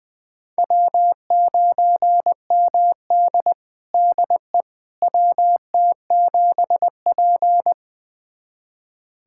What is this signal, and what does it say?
Morse code, heard as a keyed tone, about 20 wpm, 700 Hz. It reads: W9MD DE WT7P
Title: USA Callsign DE Callsign - 4 Characters - 20wpm